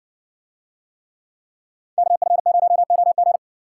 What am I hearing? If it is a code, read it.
Morse code, heard as a keyed tone, about 60 wpm, 700 Hz. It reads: KF0OG